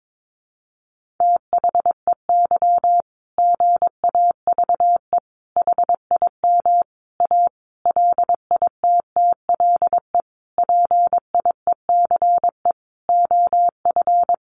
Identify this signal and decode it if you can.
Morse code, heard as a keyed tone, about 22 wpm, 700 Hz. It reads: THEY GAVE HIM A LITTLE PIECE OF
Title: Sentences from Top 1500 Words 22wpm